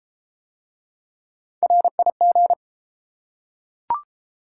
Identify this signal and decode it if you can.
Morse code, heard as a keyed tone, about 30 wpm, 700 Hz. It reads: RIG